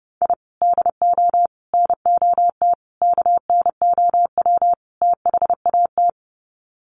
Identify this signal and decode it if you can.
Morse code, heard as a keyed tone, about 30 wpm, 700 Hz. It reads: I DO NOT KNOW THAT
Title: Sets of 5 Words 30wpm